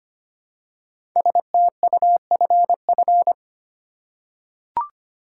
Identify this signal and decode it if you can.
Morse code, heard as a keyed tone, about 25 wpm, 700 Hz. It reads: STUFF
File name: 5 Letter Words - First 3 Letters Given - 25wpm